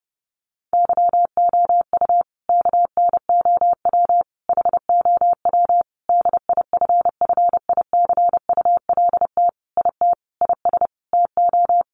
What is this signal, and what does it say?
Morse code, heard as a keyed tone, about 30 wpm, 700 Hz. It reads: YOU KNOW HOW DIFFICULT IT IS TO
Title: Sentences from Top 1100 Words 30wpm